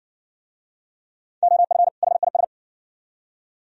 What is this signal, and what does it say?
Morse code, heard as a keyed tone, about 60 wpm, 690 Hz. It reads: OF HIS